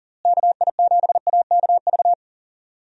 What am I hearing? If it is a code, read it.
Morse code, heard as a keyed tone, about 40 wpm, 700 Hz. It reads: KI7AKV
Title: USA Call Signs - 6 Characters - Repeated 2x - 40wpm